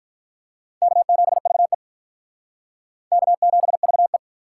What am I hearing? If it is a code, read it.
Morse code, heard as a keyed tone, about 42 wpm, 700 Hz. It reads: K7VE K7VE
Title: Speed Racing - USA 1x2 Call Signs 35wpm